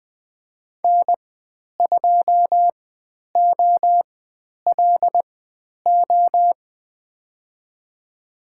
Morse code, heard as a keyed tone, about 20 wpm, spaced as for 10 wpm, 700 Hz. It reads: N2OLO
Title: 102 Lesson 17g - Letter u - Call Signs 20wpm